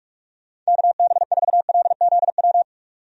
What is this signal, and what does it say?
Morse code, heard as a keyed tone, about 46 wpm, 700 Hz. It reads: KB4LZW
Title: USA Call Signs - 6 Characters - 45wpm